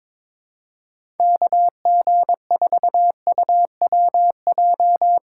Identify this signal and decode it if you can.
Morse code, heard as a keyed tone, about 22 wpm, 700 Hz. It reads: KG4UWJ
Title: USA Call Signs - 6 Characters - 6x Character Spacing - 22wpm